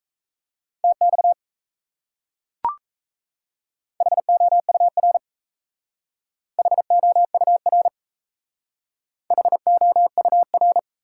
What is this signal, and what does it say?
Morse code, heard as a keyed tone, about 35 wpm, 700 Hz. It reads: TX HOUR HOUR HOUR